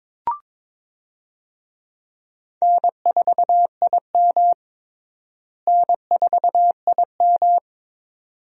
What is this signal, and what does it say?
Morse code, heard as a keyed tone, about 23 wpm, 700 Hz. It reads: N4IM N4IM
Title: USA Call Signs - 4 Characters - Repeated 2x - 22wpm